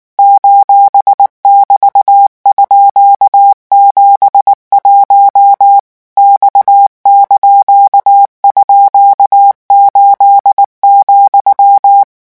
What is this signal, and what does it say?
Morse code, heard as a keyed tone, about 19 wpm, 800 Hz. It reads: :<BT>_71 X)_8,
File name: morse mp3 / revision17-vitesse 18 mots